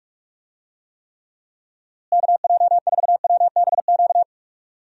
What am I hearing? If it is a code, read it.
Morse code, heard as a keyed tone, about 46 wpm, 700 Hz. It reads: KJ4WBQ
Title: USA Call Signs - 6 Characters - Repeated 2x - 45wpm